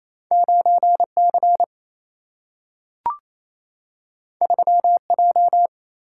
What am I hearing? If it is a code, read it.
Morse code, heard as a keyed tone, about 28 wpm, 700 Hz. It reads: 9 C 3 J
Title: Single-Letter Single-Number 28wpm